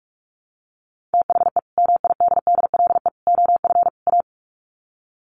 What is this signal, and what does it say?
Morse code, heard as a keyed tone, about 44 wpm, 700 Hz. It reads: THE MIDDLE OF A
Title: Sets of 4 Words 45wpm